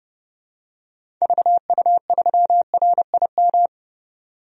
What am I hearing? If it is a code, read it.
Morse code, heard as a keyed tone, about 30 wpm, 700 Hz. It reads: VU3RIM
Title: DXCC-100 Call Signs - 6 Characters - 30wpm